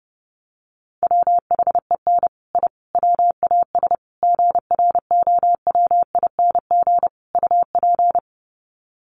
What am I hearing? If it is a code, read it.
Morse code, heard as a keyed tone, about 30 wpm, 700 Hz. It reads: WHEN I WAS GROWING UP
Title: Sets of 5 Words Encore 30wpm